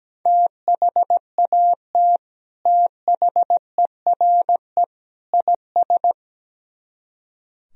Morse code, heard as a keyed tone, about 17 wpm, 700 Hz. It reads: THAT THERE IS
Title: Speed-Racing - Sets of 3 Words 17wpm